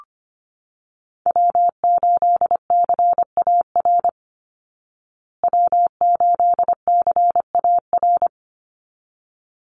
Morse code, heard as a keyed tone, about 25 wpm, 700 Hz. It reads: W8CAR W8CAR
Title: Call Signs Repeated 3x - 25wpm